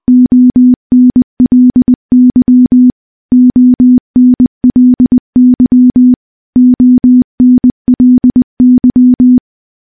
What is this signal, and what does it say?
Morse code, heard as a keyed tone, about 20 wpm, 250 Hz. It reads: ONLY ONLY ONLY
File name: morse-training-20WPM